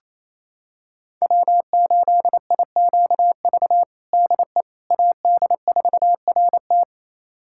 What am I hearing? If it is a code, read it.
Morse code, heard as a keyed tone, about 28 wpm, 700 Hz. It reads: W8IQV DE AD4RT
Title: USA Callsign DE Callsign 28wpm